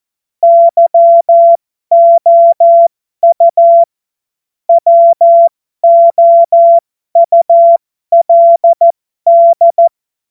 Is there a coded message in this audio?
Yes: Morse code from 0.4 to 9.9 s, heard as a keyed tone, about 14 wpm, spaced as for 12 wpm, 680 Hz. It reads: YOU WOULD